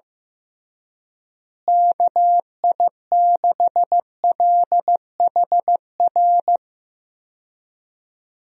Morse code, heard as a keyed tone, about 15 wpm, 700 Hz. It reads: KI6LHR
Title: USA Call Signs - 6 Characters - Repeated 3x - 15wpm